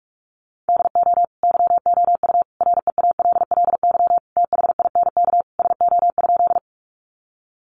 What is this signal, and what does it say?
Morse code, heard as a keyed tone, about 44 wpm, 700 Hz. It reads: DO YOU REALLY THINK SO?
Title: Sentences from Top 600 Words 45wpm